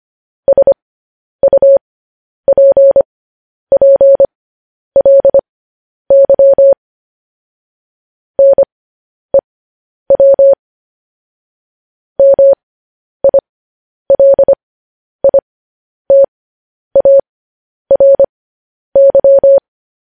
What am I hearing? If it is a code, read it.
Morse code, heard as a keyed tone, about 25 wpm, spaced as for 10 wpm, 550 Hz. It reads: SUPPLY NEW MILITARY